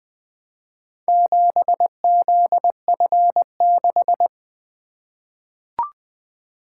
Morse code, heard as a keyed tone, about 20 wpm, 700 Hz. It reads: 7ZF6
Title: Four Letter-Number Combinations - 20wpm